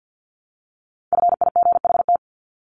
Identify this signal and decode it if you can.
Morse code, heard as a keyed tone, about 50 wpm, 700 Hz. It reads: FIGHT
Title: ING Suffix - Top 100 Words with Base Words 50wpm